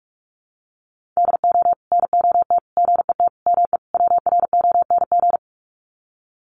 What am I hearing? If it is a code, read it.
Morse code, heard as a keyed tone, about 44 wpm, 700 Hz. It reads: DO NOT GET ME WRONG
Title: Sets of 5 Words Encore 45wpm